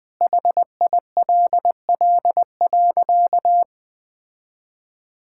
Morse code, heard as a keyed tone, about 21 wpm, 700 Hz. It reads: HILL.
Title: Sentences with 5-Word Phrases - 20wpm